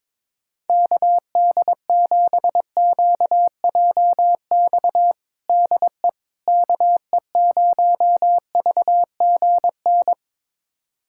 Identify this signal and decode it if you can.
Morse code, heard as a keyed tone, about 21 wpm, 700 Hz. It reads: KD7QJX DE KE0VGN
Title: USA Callsign DE Callsign 22wpm